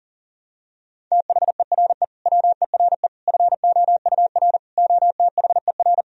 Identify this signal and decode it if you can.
Morse code, heard as a keyed tone, about 40 wpm, 700 Hz. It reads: THERE WERE FOUR OTHER
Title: Sentences from Top 1000 Words - Review Part 1 - 40wpm